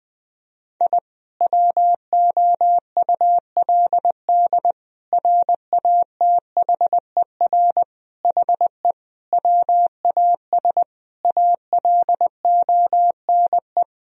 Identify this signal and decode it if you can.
Morse code, heard as a keyed tone, about 20 wpm, 700 Hz. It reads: I WOULD RATHER HE WAS ALONE
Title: Sentences from Top 1000 Words - Review Part 4 - 20wpm